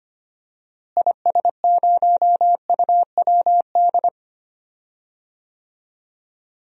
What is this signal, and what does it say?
Morse code, heard as a keyed tone, about 25 wpm, 700 Hz. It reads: IS0UWD
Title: DXCC-100 Call Signs - 6 Characters - Repeated 2x - 25wpm